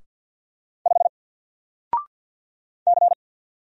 Morse code, heard as a keyed tone, about 48 wpm, 700 Hz. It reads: 5 C
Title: Single Letter-Number Alt 50wpm